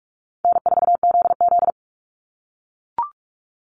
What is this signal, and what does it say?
Morse code, heard as a keyed tone, about 44 wpm, 700 Hz. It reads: N4ZZ
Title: Easy Call Signs - 45wpm